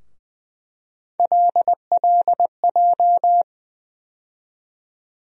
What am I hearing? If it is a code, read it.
Morse code, heard as a keyed tone, about 20 wpm, 700 Hz. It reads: LLJ
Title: USA Call Sign Postfixes - 3 Letters 20wpm